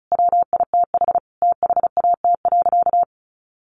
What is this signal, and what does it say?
Morse code, heard as a keyed tone, about 34 wpm, 700 Hz. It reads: WITH THAT.
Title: Sentences from Top 900 Words 35wpm